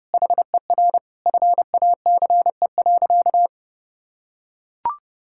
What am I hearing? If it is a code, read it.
Morse code, heard as a keyed tone, about 32 wpm, 700 Hz. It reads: HER FACE.
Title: Sentences from Top 1600 Words 30wpm